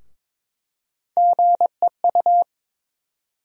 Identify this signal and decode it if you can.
Morse code, heard as a keyed tone, about 22 wpm, 700 Hz. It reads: GEU
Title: USA Call Sign Postfixes - 3 Letters 22wpm